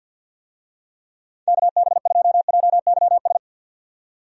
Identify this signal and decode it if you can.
Morse code, heard as a keyed tone, about 49 wpm, 700 Hz. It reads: KB2JYS
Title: USA Call Signs - 6 Characters - 50wpm